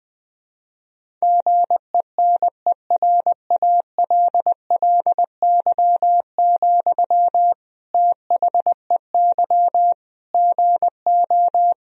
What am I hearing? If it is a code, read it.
Morse code, heard as a keyed tone, about 20 wpm, 700 Hz. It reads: GENERALLY, THEY GO
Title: Sentences from Top 1000 Words - Review Part 4 - 20wpm